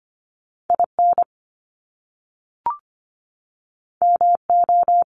Morse code, heard as a keyed tone, about 25 wpm, 700 Hz. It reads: IN MO
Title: US States 25wpm